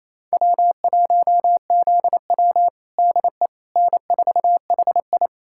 Morse code, heard as a keyed tone, about 28 wpm, 700 Hz. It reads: W1ZW DE N4HI
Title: USA Callsign DE Callsign - 4 Characters - 28wpm